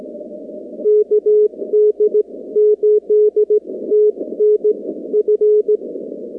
Morse code, heard as a keyed tone, about 16 wpm, spaced as for 14 wpm, 440 Hz. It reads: KD8TNF
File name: contests / 2019-cw-wpx-kt5j-k5pi- -2330 0-45